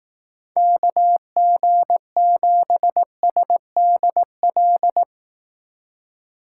Morse code, heard as a keyed tone, about 18 wpm, 700 Hz. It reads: KG7SDL
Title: Speed Racing - USA Call Signs 15wpm